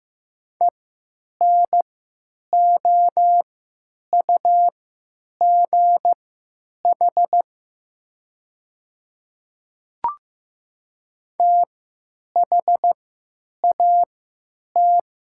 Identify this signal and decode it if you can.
Morse code, heard as a keyed tone, about 15 wpm, spaced as for 9 wpm, 700 Hz. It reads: ENOUGH THAT
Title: 3x Character Spacing - Top 500 Words - 15wpm